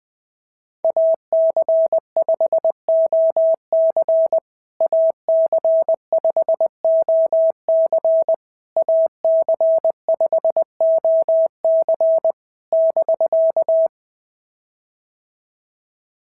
Morse code, heard as a keyed tone, about 21 wpm, 650 Hz. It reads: AC5OC AC5OC AC5OC <BK>
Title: POTA QSO Practice - 20wpm